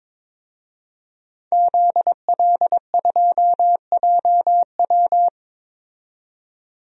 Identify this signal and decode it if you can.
Morse code, heard as a keyed tone, about 22 wpm, 700 Hz. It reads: ZL2JW
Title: DXCC-100 Call Signs - 5 Characters - 22wpm